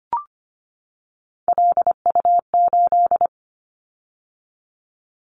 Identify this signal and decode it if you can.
Morse code, heard as a keyed tone, about 25 wpm, 700 Hz. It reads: LU8
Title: Callsign Prefixes for Top 100 DXCC - 25wpm